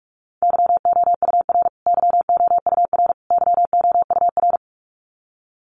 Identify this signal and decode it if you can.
Morse code, heard as a keyed tone, about 44 wpm, 700 Hz. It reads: YOUR YOUR YOUR
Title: Top 500 Words Repeated 45wpm